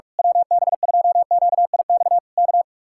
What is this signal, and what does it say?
Morse code, heard as a keyed tone, about 46 wpm, 700 Hz. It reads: WB2QIX K
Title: Speed-Racing Common QSO Phrases - 45wpm